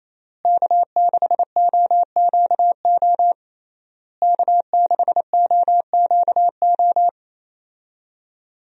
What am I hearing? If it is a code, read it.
Morse code, heard as a keyed tone, about 28 wpm, 700 Hz. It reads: K6OQO K6OQO
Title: USA Call Signs - 5 Characters - Repeated 3x - 28wpm